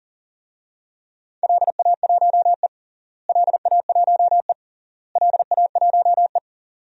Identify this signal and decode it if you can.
Morse code, heard as a keyed tone, about 40 wpm, 700 Hz. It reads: LA1E LA1E LA1E
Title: DXCC-100 Call Signs - 4 Characters - Repeated 3x - 40wpm